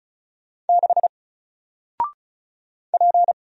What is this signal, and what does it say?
Morse code, heard as a keyed tone, about 36 wpm, 700 Hz. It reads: 6 P
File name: Single-Letter-Number-35wpm